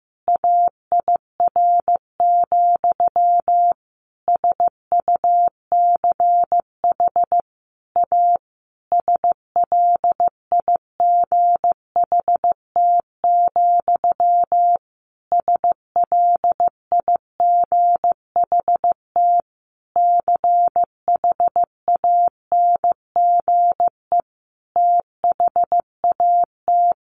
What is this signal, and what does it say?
Morse code, heard as a keyed tone, about 15 wpm, 700 Hz. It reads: AIR, SUCH A SLIGHT, SLIGHT CHANGE THAT
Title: A Sound Of Thunder 15wpm